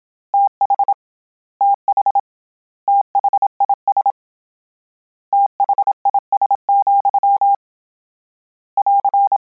Morse code, heard as a keyed tone, about 27 wpm, 800 Hz. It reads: TH TH THIS THIS, <AR>
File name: Combinations / Combinations-mp3s / Combinations@25